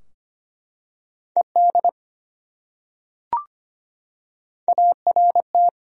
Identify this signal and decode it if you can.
Morse code, heard as a keyed tone, about 25 wpm, 700 Hz. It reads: ED ART
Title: USA Common Names - Max 3 Letters - 25wpm